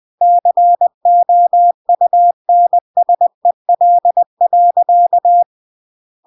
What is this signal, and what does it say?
Morse code, heard as a keyed tone, about 20 wpm, 700 Hz. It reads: COUNSEL.
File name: CW / jodorowsky / 82 commandments 20-20